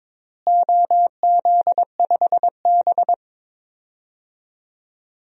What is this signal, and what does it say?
Morse code, heard as a keyed tone, about 22 wpm, 700 Hz. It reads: OZ5B
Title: DXCC-100 Call Signs - 4 Characters - Repeated 3x - 22wpm